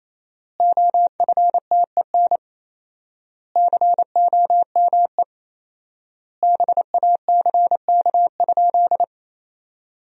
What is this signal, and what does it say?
Morse code, heard as a keyed tone, about 28 wpm, 700 Hz. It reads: OFTEN COME BACK?
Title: 4x Word-Spacing - Sentences from Top 300 Words - 28wpm